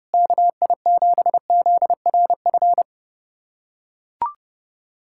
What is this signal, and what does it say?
Morse code, heard as a keyed tone, about 30 wpm, 700 Hz. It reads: KI7ZRF
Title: USA Callsign DE Callsign 30wpm